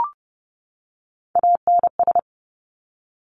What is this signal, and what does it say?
Morse code, heard as a keyed tone, about 30 wpm, 690 Hz. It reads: ANS
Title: Top Three Letter Combinations in Words 30wpm